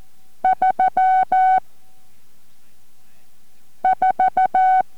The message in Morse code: 3 4